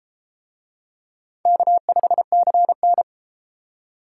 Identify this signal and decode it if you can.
Morse code, heard as a keyed tone, about 33 wpm, 700 Hz. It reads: K5CN